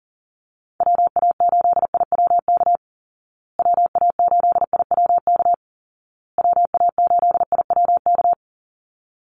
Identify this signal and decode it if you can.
Morse code, heard as a keyed tone, about 40 wpm, 700 Hz. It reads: WA8IWK WA8IWK WA8IWK
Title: Call Signs Repeated 2x - 40wpm